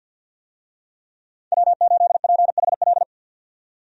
Morse code, heard as a keyed tone, about 53 wpm, 700 Hz. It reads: W8PHL
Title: USA Call Signs - 5 Characters - 50wpm